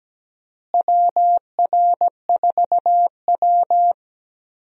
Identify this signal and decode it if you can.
Morse code, heard as a keyed tone, about 17 wpm, 700 Hz. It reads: WR4W
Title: USA Call Signs - 4 Characters - Repeated 3x - 17wpm